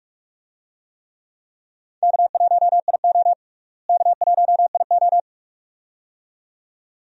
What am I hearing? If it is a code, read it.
Morse code, heard as a keyed tone, about 46 wpm, 700 Hz. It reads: K1IO K1IO
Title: USA Call Signs - 4 Characters - Repeated 2x - 45wpm